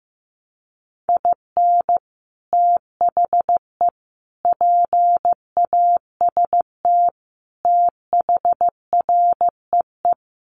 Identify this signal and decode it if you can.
Morse code, heard as a keyed tone, about 15 wpm, 700 Hz. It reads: IN THE PAST THREE
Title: Sets of 5 Words Encore2 15wpm